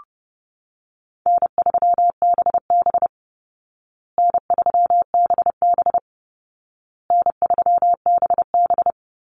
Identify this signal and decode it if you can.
Morse code, heard as a keyed tone, about 30 wpm, 700 Hz. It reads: N3BB N3BB N3BB
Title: Easy Call Signs Repeated 3x - 30wpm